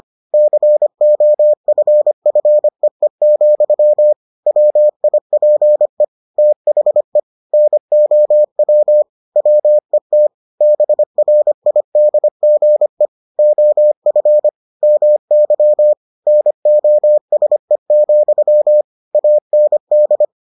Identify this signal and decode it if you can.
Morse code, heard as a keyed tone, about 25 wpm, 600 Hz. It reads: COFFEE, WIPE THE NOW WET BRIDGE OF MY NOSE, AND